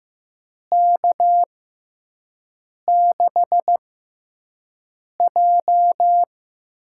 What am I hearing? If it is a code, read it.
Morse code, heard as a keyed tone, about 15 wpm, spaced as for 5 wpm, 700 Hz. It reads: K6J
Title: USA Call Signs - 6x Character Spacing - 15wpm